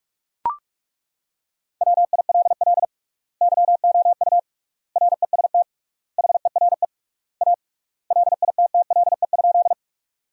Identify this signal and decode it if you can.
Morse code, heard as a keyed tone, about 42 wpm, 700 Hz. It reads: WILL YOU REST HERE A LITTLE?
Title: 3x Word-Spacing - Sentences from Top 500 Words - 45wpm